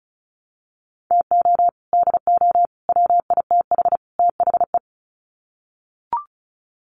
Morse code, heard as a keyed tone, about 34 wpm, 700 Hz. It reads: TO DO WITH THE
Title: Sets of 4 Words 35wpm